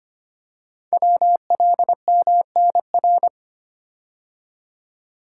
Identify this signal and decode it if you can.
Morse code, heard as a keyed tone, about 25 wpm, 700 Hz. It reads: WLMNR
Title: Code Groups - 5 Letters - 25wpm